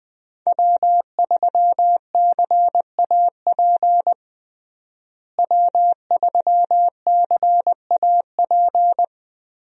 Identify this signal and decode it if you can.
Morse code, heard as a keyed tone, about 21 wpm, 700 Hz. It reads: W3CAP W3CAP
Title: USA Call Signs - 5 Characters - Repeated 3x - 20wpm